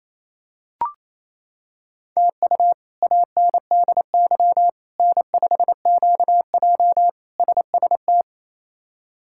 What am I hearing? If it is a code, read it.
Morse code, heard as a keyed tone, about 27 wpm, 700 Hz. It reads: TU ANDY N5QJ SST
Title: K1USN SST QSO Practice - 28wpm